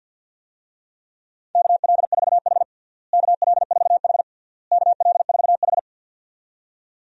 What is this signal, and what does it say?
Morse code, heard as a keyed tone, about 52 wpm, 700 Hz. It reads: KL4H KL4H KL4H
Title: USA Call Signs - 4 Characters - Repeated 3x - 50wpm